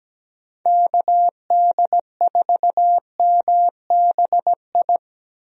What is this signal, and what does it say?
Morse code, heard as a keyed tone, about 17 wpm, 700 Hz. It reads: KD4MBI
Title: USA Call Signs - 6 Characters - Repeated 2x - 17wpm